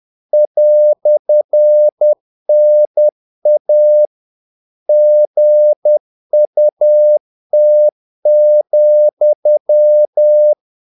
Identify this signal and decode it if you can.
Morse code, heard as a keyed tone, about 10 wpm, 600 Hz. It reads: "NA GUT,
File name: dk5ke / Humor-50BpM